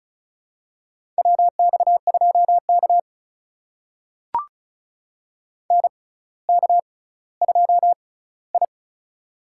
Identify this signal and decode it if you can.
Morse code, heard as a keyed tone, about 36 wpm, spaced as for 18 wpm, 700 Hz. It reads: WX2K NK2I
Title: USA Call Signs - 4 Characters - 6x Character Spacing - 35wpm